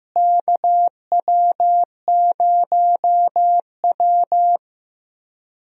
Morse code, heard as a keyed tone, about 15 wpm, 700 Hz. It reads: KW0W
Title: USA Call Signs - 4 Characters - Repeated 2x - 15wpm